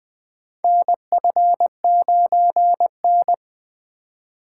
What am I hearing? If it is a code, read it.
Morse code, heard as a keyed tone, about 21 wpm, 700 Hz. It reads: NF9N
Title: USA Call Signs - 4 Characters - 4x Character Spacing - 20wpm